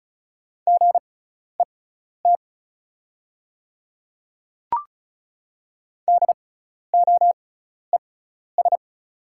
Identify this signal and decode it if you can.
Morse code, heard as a keyed tone, about 36 wpm, spaced as for 12 wpm, 700 Hz. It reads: GET DOES
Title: 6x Character Spacing - Top 200 Words - 35wpm